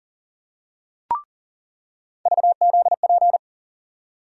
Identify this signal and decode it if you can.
Morse code, heard as a keyed tone, about 38 wpm, 700 Hz. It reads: VZP
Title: USA Call Sign Postfixes - 3 Letters 40wpm